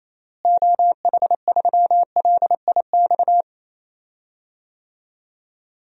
Morse code, heard as a keyed tone, about 28 wpm, 700 Hz. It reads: OH3LIX
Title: DXCC-100 Call Signs - 6 Characters - Repeated 2x - 28wpm